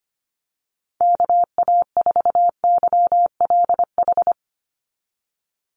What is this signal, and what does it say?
Morse code, heard as a keyed tone, about 25 wpm, 690 Hz. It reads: KA4YLH
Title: USA Call Signs 25wpm